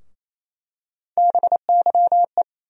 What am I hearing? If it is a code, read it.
Morse code, heard as a keyed tone, about 28 wpm, 700 Hz. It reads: BYE
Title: USA Call Sign Postfixes - 3 Letters 28wpm